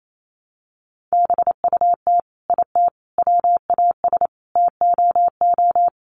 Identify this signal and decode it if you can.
Morse code, heard as a keyed tone, about 28 wpm, 700 Hz. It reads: BUT IT WAS TOO
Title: Sets of 5 Words Encore2 28wpm